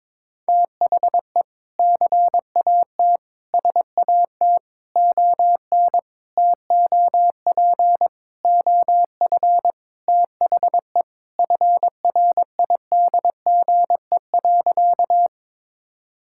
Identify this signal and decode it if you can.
Morse code, heard as a keyed tone, about 22 wpm, 700 Hz. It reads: THE CAT SAT ON TOP OF THE FRIDGE.
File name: practice / 22wpm / Sentences with 4-Word Phrases - 22wpm